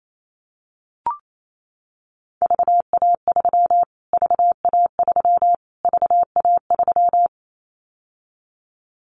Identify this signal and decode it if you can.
Morse code, heard as a keyed tone, about 28 wpm, 700 Hz. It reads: VA3 VA3 VA3
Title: Callsign Prefixes for Top 25 DXCC - Repeated 3x - 28wpm